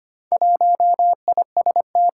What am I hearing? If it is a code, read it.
Morse code, heard as a keyed tone, about 25 wpm, 700 Hz. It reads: 1IST
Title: Four Challenging Letter-Number Combinations 25wpm